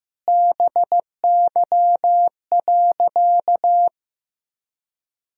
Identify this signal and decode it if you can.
Morse code, heard as a keyed tone, about 15 wpm, 700 Hz. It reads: BY.
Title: Sentences from Top 1000 Words - Review Part 4 - 15wpm